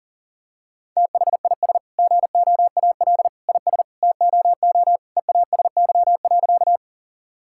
Morse code, heard as a keyed tone, about 37 wpm, 700 Hz. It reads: THIS GOAL IS TOO EASY.
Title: Sentences from Top 1000 Words - Review Part 5 - 40wpm